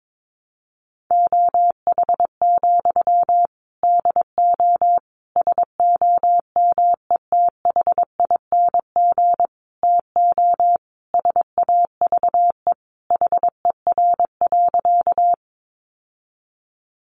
Morse code, heard as a keyed tone, about 22 wpm, 700 Hz. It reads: OH, DO SOMETHING TO SAVE HER.
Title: Sentences from Top 1100 Words 22wpm